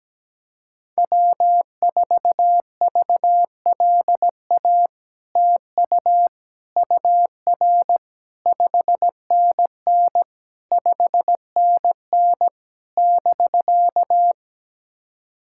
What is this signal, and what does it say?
Morse code, heard as a keyed tone, about 17 wpm, 700 Hz. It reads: W4VLA TU UR 5NN 5NN <BK>
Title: SOTA QSO Practice 17wpm